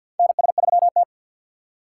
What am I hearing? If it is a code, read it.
Morse code, heard as a keyed tone, about 53 wpm, 700 Hz. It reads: NS3T
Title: USA Call Signs - 4 Characters - 2x Character Spacing - 50wpm